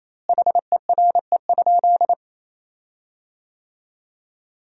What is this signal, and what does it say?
Morse code, heard as a keyed tone, about 28 wpm, 700 Hz. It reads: HERE?
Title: 4x Word-Spacing - Sentences from Top 500 Words - 28wpm